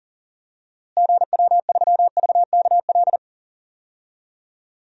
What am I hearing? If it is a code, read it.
Morse code, heard as a keyed tone, about 40 wpm, 700 Hz. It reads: GW3VKL
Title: DXCC-100 Call Signs - 6 Characters - Repeated 3x - 40wpm